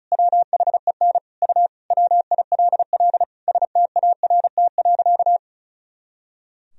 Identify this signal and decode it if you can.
Morse code, heard as a keyed tone, about 36 wpm, 700 Hz. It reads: WHEN U WILL START.
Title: Sentences from Top 300 Words and Abbreviations 35wpm